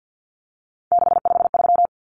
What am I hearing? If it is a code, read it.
Morse code, heard as a keyed tone, about 50 wpm, 700 Hz. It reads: 653
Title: Numbers - 3-Digit 50wpm